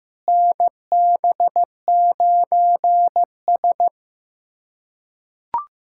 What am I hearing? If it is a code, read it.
Morse code, heard as a keyed tone, about 15 wpm, 700 Hz. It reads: NB9S
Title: USA Callsign DE Callsign - 4 Characters - 15wpm